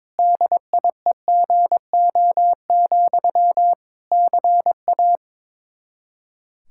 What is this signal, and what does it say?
Morse code, heard as a keyed tone, about 22 wpm, 700 Hz. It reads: DIEGO, CA
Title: Practice QSO - Exchange 1 - Beginning - 22wpm